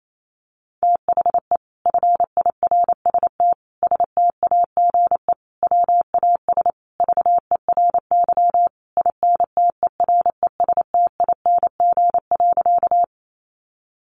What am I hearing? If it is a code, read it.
Morse code, heard as a keyed tone, about 28 wpm, 700 Hz. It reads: THE FIRST STAGE WAS VERY INTERESTING.
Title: Sentences from Top 1200 Words 28wpm